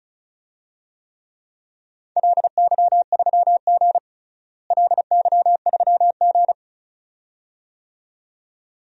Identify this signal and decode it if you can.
Morse code, heard as a keyed tone, about 34 wpm, 700 Hz. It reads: LY3G LY3G
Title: DXCC-100 Call Signs - 4 Characters - Repeated 2x - 35wpm